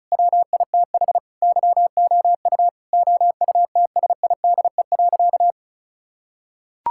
Morse code, heard as a keyed tone, about 34 wpm, 700 Hz. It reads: WITH YOU OUTSIDE.
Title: Sentences from Top 1000 Words - Review Part 6 - 35wpm